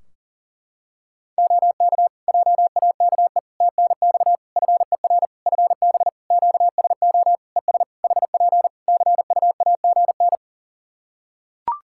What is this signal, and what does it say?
Morse code, heard as a keyed tone, about 40 wpm, 700 Hz. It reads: OK JAKE TNX FER FB QSO ES HP CUAGN
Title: Practice QSO - Exchange 4 - Ending - 40wpm